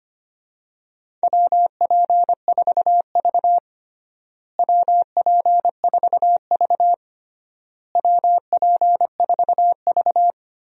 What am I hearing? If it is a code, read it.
Morse code, heard as a keyed tone, about 25 wpm, 700 Hz. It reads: WP4V WP4V WP4V
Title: USA Call Signs - 4 Characters - Repeated 3x - 25wpm